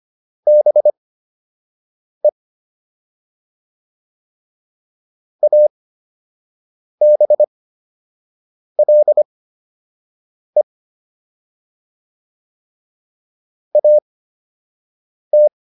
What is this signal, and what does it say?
Morse code, heard as a keyed tone, about 25 wpm, spaced as for 6 wpm, 600 Hz. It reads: BE ABLE AT